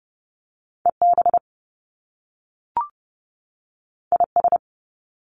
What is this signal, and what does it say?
Morse code, heard as a keyed tone, about 30 wpm, 700 Hz. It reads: EB IS
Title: Top Two Letter Combinations in Words 30wpm